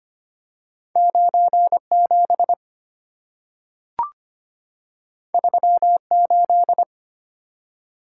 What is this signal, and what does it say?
Morse code, heard as a keyed tone, about 25 wpm, 700 Hz. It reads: 97 38
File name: Two Letter-Number Combinations - 25wpm